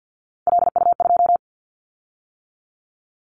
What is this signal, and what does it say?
Morse code, heard as a keyed tone, about 50 wpm, 700 Hz. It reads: LU2
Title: Callsign Prefixes for Top 100 DXCC - 50wpm